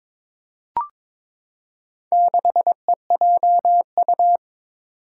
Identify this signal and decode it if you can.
Morse code, heard as a keyed tone, about 22 wpm, 700 Hz. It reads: 6EJU
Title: Four Challenging Letter-Number Combinations 22wpm